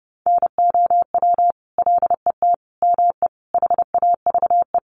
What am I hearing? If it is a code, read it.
Morse code, heard as a keyed tone, about 30 wpm, 700 Hz. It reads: NOW LET ME HAVE
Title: Sentences from Top 300 Words 30wpm